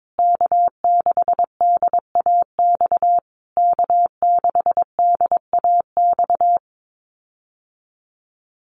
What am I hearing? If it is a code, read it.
Morse code, heard as a keyed tone, about 22 wpm, 700 Hz. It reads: K6DAX K6DAX
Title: USA Call Signs Repeated 3x 22wpm